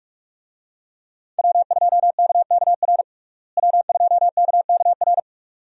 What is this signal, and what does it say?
Morse code, heard as a keyed tone, about 46 wpm, 700 Hz. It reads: W2KKR W2KKR
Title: USA Call Signs - 5 Characters - Repeated 2x - 45wpm